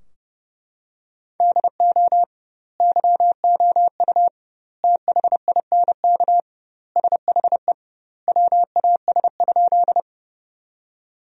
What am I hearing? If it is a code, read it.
Morse code, heard as a keyed tone, about 29 wpm, 700 Hz. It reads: DO YOU THINK SHE WAS?
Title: 2x Word-Spacing - Sentences from Top 200 Words - 30wpm